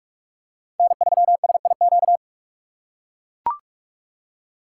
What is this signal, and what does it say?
Morse code, heard as a keyed tone, about 43 wpm, 700 Hz. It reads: N3SIQ